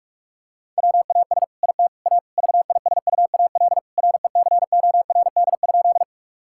Morse code, heard as a keyed tone, about 46 wpm, 700 Hz. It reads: WAS IT A VISUAL RECORD?
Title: Sentences from Top 1000 Words - Review Part 6 - 45wpm